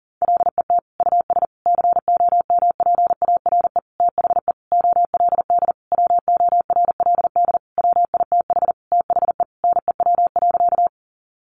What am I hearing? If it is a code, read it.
Morse code, heard as a keyed tone, about 40 wpm, 700 Hz. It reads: LET US COMPARE THE OLD WORLD WITH THE NEW.
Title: Sentences from Top 1300 Words 40wpm